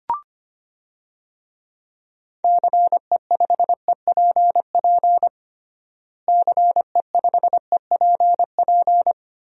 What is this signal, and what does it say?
Morse code, heard as a keyed tone, about 25 wpm, 700 Hz. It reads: CE5EPP CE5EPP
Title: DXCC-100 Call Signs - 6 Characters - Repeated 3x - 25wpm